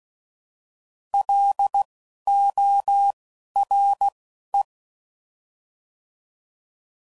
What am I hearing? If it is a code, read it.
Morse code, heard as a keyed tone, about 16 wpm, spaced as for 11 wpm, 800 Hz. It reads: LORE
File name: Prefixes / By First Letter / Ls